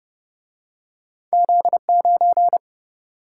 Morse code, heard as a keyed tone, about 29 wpm, 700 Hz. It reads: Z9